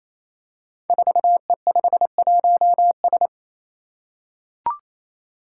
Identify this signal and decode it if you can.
Morse code, heard as a keyed tone, about 29 wpm, 700 Hz. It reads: 4E51S
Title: Five Challenging Letter-Number Combinations 28wpm